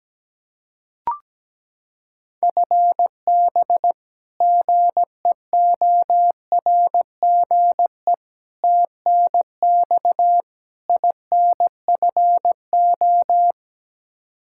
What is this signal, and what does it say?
Morse code, heard as a keyed tone, about 17 wpm, 700 Hz. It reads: FB GEORGE TNX INFO
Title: K1USN SST QSO Practice 17wpm